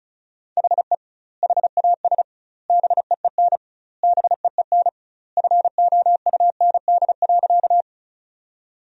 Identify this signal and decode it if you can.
Morse code, heard as a keyed tone, about 36 wpm, 700 Hz. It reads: HE HAS BEEN BEEN FOUND.
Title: 2x Word-Spacing - Sentences from Top 300 Words - 35wpm